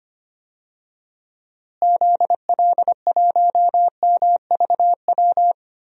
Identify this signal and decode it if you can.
Morse code, heard as a keyed tone, about 25 wpm, 700 Hz. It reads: ZL1MVW
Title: DXCC-100 Call Signs - 6 Characters - Repeated 3x - 25wpm